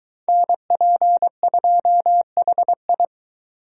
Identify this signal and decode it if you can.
Morse code, heard as a keyed tone, about 23 wpm, 700 Hz. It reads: NP2HI